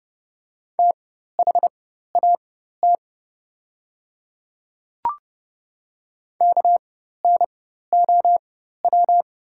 Morse code, heard as a keyed tone, about 31 wpm, spaced as for 14 wpm, 700 Hz. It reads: THAT KNOW
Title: 4x Character Spacing - Top 500 Words - 30wpm